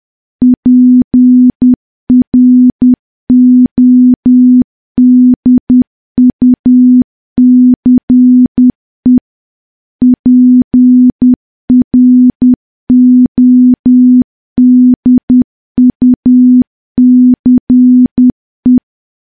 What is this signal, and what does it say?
Morse code, heard as a keyed tone, about 10 wpm, 250 Hz. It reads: PRODUCE PRODUCE
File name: morse / morse-training-10WPM